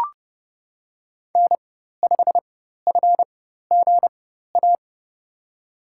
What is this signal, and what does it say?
Morse code, heard as a keyed tone, about 30 wpm, spaced as for 14 wpm, 700 Hz. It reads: N5FGA
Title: USA Call Signs - 5 Characters - 4x Character Spacing - 30wpm